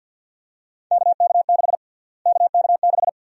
Morse code, heard as a keyed tone, about 52 wpm, 700 Hz. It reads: KK6 KK6
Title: USA Call Sign Prefixes - Repeated 2x - 50wpm